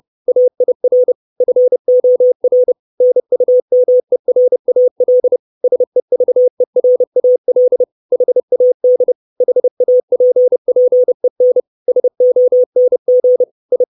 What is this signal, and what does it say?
Morse code, heard as a keyed tone, about 30 wpm, 500 Hz. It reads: AIR FOR NUMERAL SEVERAL HAD HAPPEN SONG I
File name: 632 Words 30WPM